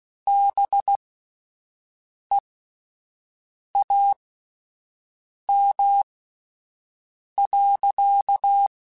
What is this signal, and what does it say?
Morse code, heard as a keyed tone, about 16 wpm, spaced as for 5 wpm, 800 Hz. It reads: BEAM.
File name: QSOs / QSOs-mp3s / WD0BGP 1@13